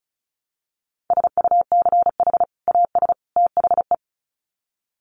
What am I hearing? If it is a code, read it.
Morse code, heard as a keyed tone, about 34 wpm, 700 Hz. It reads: SUCH AS THE
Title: Sets of 3 Words 35wpm